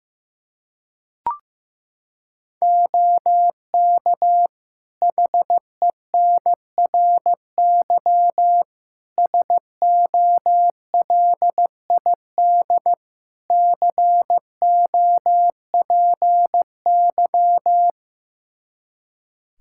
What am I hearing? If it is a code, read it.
Morse code, heard as a keyed tone, about 15 wpm, 700 Hz. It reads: OK HENRY SOLID COPY
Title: Practice QSO - Exchange 3 - More Info - 15wpm